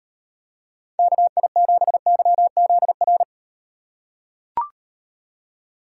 Morse code, heard as a keyed tone, about 38 wpm, 700 Hz. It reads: KI7YZR